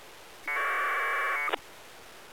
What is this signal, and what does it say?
Packet radio (AX.25, as in APRS).